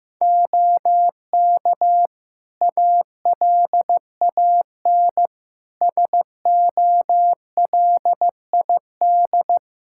Morse code, heard as a keyed tone, about 15 wpm, 700 Hz. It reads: OK ALAN SOLID
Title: Practice QSO - Exchange 3 - More Info - 15wpm